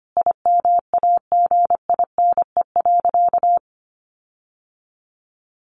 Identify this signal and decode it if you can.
Morse code, heard as a keyed tone, about 25 wpm, 700 Hz. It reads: IMAGINE.
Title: Sentences from Top 1400 Words 25wpm